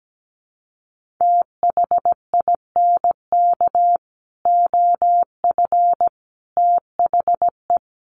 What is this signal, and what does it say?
Morse code, heard as a keyed tone, about 17 wpm, 700 Hz. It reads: THINK OF THE
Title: Sentences from Top 1500 Words 17wpm